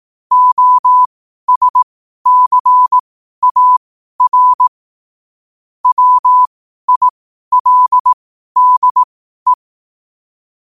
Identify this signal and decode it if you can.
Morse code, heard as a keyed tone, about 18 wpm, spaced as for 12 wpm, 1000 Hz. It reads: OSCAR WILDE
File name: qotdfiles / 12wpm